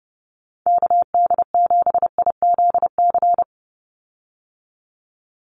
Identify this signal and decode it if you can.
Morse code, heard as a keyed tone, about 29 wpm, 700 Hz. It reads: KD7IZC
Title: USA Call Signs 30wpm